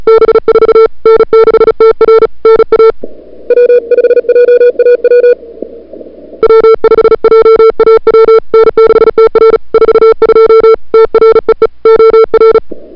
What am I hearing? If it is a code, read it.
Morse code, heard as a keyed tone, about 36 wpm, 440 Hz. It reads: B4 N6TR NA W5JAW W5JAW N6TR 42 TREE OR
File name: CWSprint / February 2007 CW Sprint / 01-2007 CW Sprint Hour1